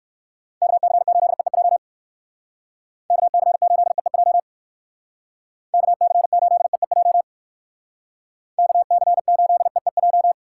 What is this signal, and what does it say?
Morse code, heard as a keyed tone, about 59 wpm, 700 Hz. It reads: KC8EEJ KC8EEJ KC8EEJ KC8EEJ